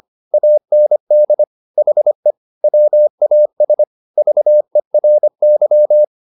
Morse code, heard as a keyed tone, about 25 wpm, 600 Hz. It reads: AND HE WAS VERY